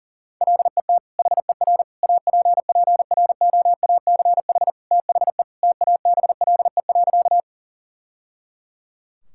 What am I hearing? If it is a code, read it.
Morse code, heard as a keyed tone, about 40 wpm, 700 Hz. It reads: LET HER APPROACH THE TABLE.
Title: Sentences from Top 1000 Words - Review Part 6 - 40wpm